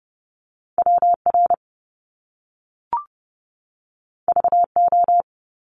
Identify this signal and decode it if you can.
Morse code, heard as a keyed tone, about 30 wpm, 700 Hz. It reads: WR VO
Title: Top Two Letter Combinations in Words 30wpm